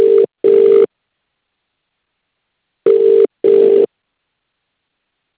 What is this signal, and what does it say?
Ringback tone (UK), 400 + 450 Hz.